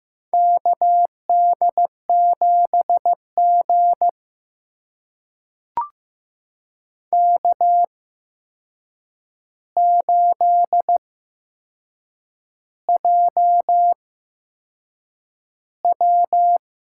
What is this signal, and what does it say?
Morse code, heard as a keyed tone, about 15 wpm, spaced as for 6 wpm, 700 Hz. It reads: KD7G K8JW
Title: USA Call Signs - 8x Character Spacing - 15wpm